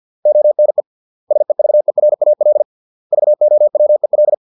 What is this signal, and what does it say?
Morse code, heard as a keyed tone, about 50 wpm, 600 Hz. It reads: ONE SEVERAL VOWEL